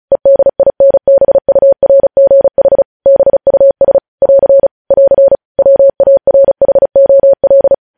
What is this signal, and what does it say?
Morse code, heard as a keyed tone, about 36 wpm, 550 Hz. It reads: EDINBURGH BUS <AR> <AR> WARHOL
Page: News headlines in Morse code at 35 wpm